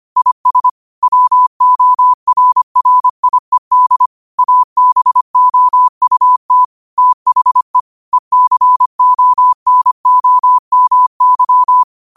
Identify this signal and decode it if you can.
Morse code, heard as a keyed tone, about 25 wpm, 1000 Hz. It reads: IS WORRIED ABOUT THE ECONOMY
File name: qotdfiles / 25wpm